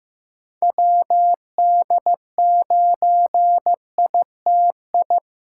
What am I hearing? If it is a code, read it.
Morse code, heard as a keyed tone, about 15 wpm, 700 Hz. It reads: WD9ITI
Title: USA Callsign DE Callsign - 6 Characters - 15wpm